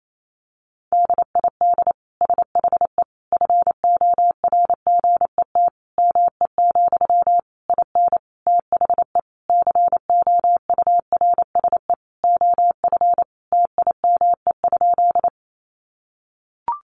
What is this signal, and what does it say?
Morse code, heard as a keyed tone, about 28 wpm, 700 Hz. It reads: DID SHE FORGET ME, IN THE COURSE OF TIME?
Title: Sentences from Top 1200 Words 28wpm